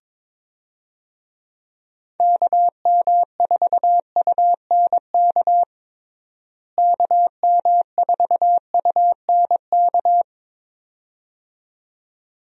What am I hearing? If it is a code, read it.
Morse code, heard as a keyed tone, about 22 wpm, 700 Hz. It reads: KM4UNK KM4UNK
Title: USA Call Signs - 6 Characters - Repeated 2x - 22wpm